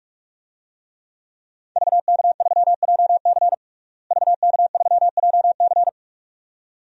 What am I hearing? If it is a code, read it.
Morse code, heard as a keyed tone, about 46 wpm, 700 Hz. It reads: VK3JC VK3JC
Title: DXCC-100 Call Signs - 5 Characters - Repeated 2x - 45wpm